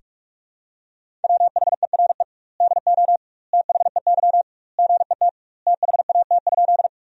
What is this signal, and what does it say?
Morse code, heard as a keyed tone, about 46 wpm, 700 Hz. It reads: WHERE DO THEY GET THAT?
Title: 2x Word-Spacing - Sentences from Top 200 Words - 45wpm